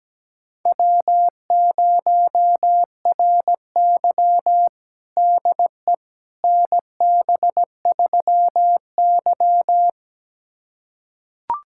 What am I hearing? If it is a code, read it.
Morse code, heard as a keyed tone, about 17 wpm, 700 Hz. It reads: W0RY DE NB3Y
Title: USA Callsign DE Callsign - 4 Characters - 17wpm